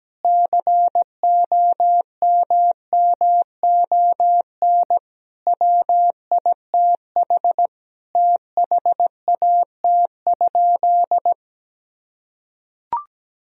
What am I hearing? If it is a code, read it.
Morse code, heard as a keyed tone, about 17 wpm, 700 Hz. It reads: COMMON WITH THAT?
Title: Sentences from Top 1000 Words - Review Part 4 - 17wpm